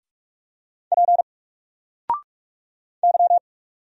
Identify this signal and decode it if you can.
Morse code, heard as a keyed tone, about 46 wpm, 700 Hz. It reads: P Y
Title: Single Letter-Number 45wpm